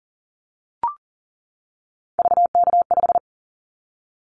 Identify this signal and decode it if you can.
Morse code, heard as a keyed tone, about 40 wpm, 700 Hz. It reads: VK5
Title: Callsign Prefixes for Top 100 DXCC - 40wpm